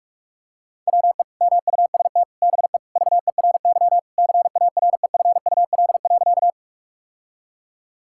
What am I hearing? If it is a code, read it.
Morse code, heard as a keyed tone, about 43 wpm, 700 Hz. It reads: WE MUST BE VERY CAREFUL.
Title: Sentences from Top 1600 Words 45wpm